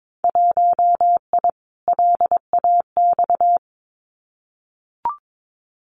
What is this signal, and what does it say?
Morse code, heard as a keyed tone, about 22 wpm, 700 Hz. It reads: 1I LAX
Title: Winter Field Day Exchanges 22wpm